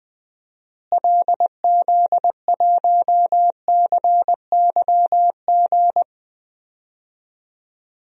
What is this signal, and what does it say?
Morse code, heard as a keyed tone, about 20 wpm, 700 Hz. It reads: LZ1CYG
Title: DXCC-100 Call Signs - 6 Characters - Repeated 2x - 20wpm